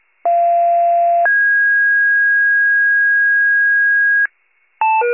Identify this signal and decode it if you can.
Two-tone page, heard as a one-second tone, then a three-second tone, about 690 then 1650 Hz. It is heard from 0.2 to 4.3 s.